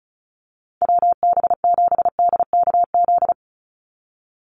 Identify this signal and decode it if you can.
Morse code, heard as a keyed tone, about 34 wpm, 700 Hz. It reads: WB7DKZ
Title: Call Signs - 35wpm